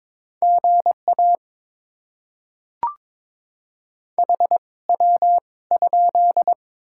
Morse code, heard as a keyed tone, about 22 wpm, spaced as for 18 wpm, 690 Hz. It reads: GA HW?
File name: Top 20 QSO Elements - 2x Character Spacing - 22wpm